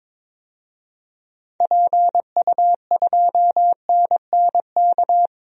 Morse code, heard as a keyed tone, about 22 wpm, 700 Hz. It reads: PU2NNK
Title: DXCC-100 Call Signs - 6 Characters - Repeated 2x - 22wpm